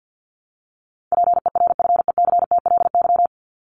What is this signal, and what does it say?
Morse code, heard as a keyed tone, about 50 wpm, 700 Hz. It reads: PERFECTLY
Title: LY Suffix - Top 100 Words with Base Words 50wpm